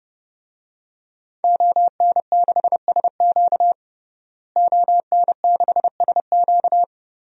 Morse code, heard as a keyed tone, about 32 wpm, 700 Hz. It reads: ON6SQ ON6SQ
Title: DXCC-100 Call Signs - 5 Characters - Repeated 3x - 30wpm